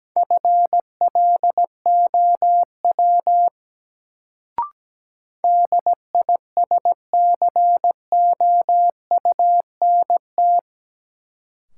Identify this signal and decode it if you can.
Morse code, heard as a keyed tone, about 17 wpm, 700 Hz. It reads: FLOW DISCOUNT
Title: Sentences from Top 1200 Words - New Word Drill Alt - 17wpm